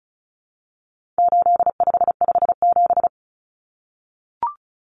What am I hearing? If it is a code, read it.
Morse code, heard as a keyed tone, about 34 wpm, 700 Hz. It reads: 8557
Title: Numbers - 4-Digit 35wpm